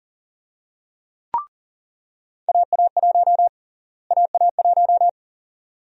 Morse code, heard as a keyed tone, about 40 wpm, 700 Hz. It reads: AA1 AA1
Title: USA Call Sign Prefixes - Repeated 2x - 40wpm